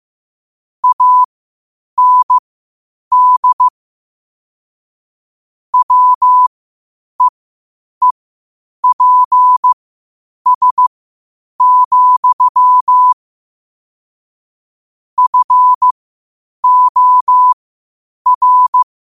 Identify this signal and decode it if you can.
Morse code, heard as a keyed tone, about 15 wpm, spaced as for 8 wpm, 1000 Hz. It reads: AND WEEPS, FOR
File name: qotdfiles / 8wpm